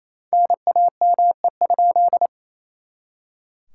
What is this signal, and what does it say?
Morse code, heard as a keyed tone, about 37 wpm, 700 Hz. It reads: NAME?